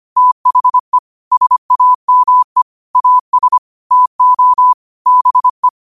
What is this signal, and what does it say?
Morse code, heard as a keyed tone, about 25 wpm, 1000 Hz. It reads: THE SAME AS TO BE
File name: qotdfiles / 25wpm